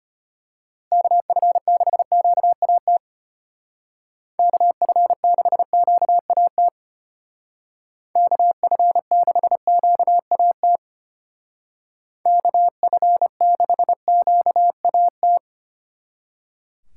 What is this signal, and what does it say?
Morse code, heard as a keyed tone, about 31 wpm, 700 Hz. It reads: KF6QAT KF6QAT KF6QAT KF6QAT
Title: Speed Racing - USA 2x3 Call Signs 25wpm